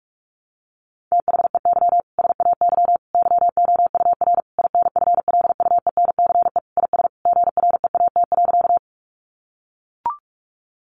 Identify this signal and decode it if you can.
Morse code, heard as a keyed tone, about 44 wpm, 700 Hz. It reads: THEY SAY YOUR INFLUENCE IS GREAT.
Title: Sentences from Top 1100 Words 45wpm